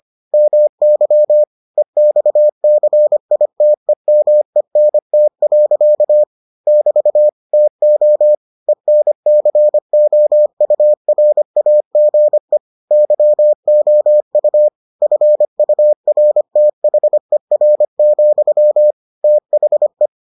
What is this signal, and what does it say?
Morse code, heard as a keyed tone, about 25 wpm, 600 Hz. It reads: MY EXCITEMENT. <BT> TO ENCOURAGE YOU FURTHER, THE